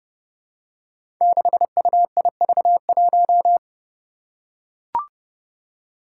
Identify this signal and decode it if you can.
Morse code, heard as a keyed tone, about 29 wpm, 700 Hz. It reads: 6UIV1
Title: Code Groups - 5 Letter:Numbers - 30wpm